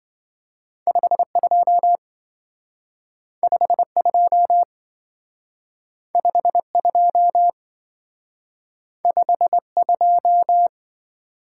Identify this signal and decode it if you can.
Morse code, heard as a keyed tone, about 25 wpm, 700 Hz. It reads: 52 52 52 52
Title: Speed-Racing - Two Letter-Number Combinations - 20wpm